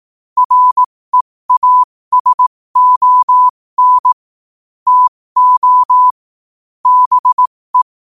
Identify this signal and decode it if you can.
Morse code, heard as a keyed tone, about 18 wpm, spaced as for 15 wpm, 1000 Hz. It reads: REASON TO BE
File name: qotdfiles / 15wpm